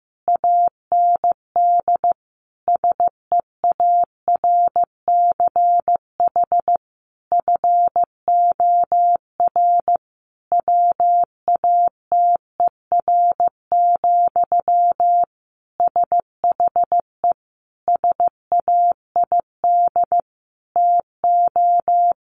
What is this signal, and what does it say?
Morse code, heard as a keyed tone, about 15 wpm, 700 Hz. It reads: AND SEARCH FOR WATER, SHE SAID TO
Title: Sentences from Top 800 Words 15wpm